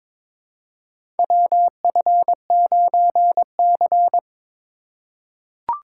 Morse code, heard as a keyed tone, about 21 wpm, 700 Hz. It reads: WF9C
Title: USA Call Signs - 4 Characters - 4x Character Spacing - 22wpm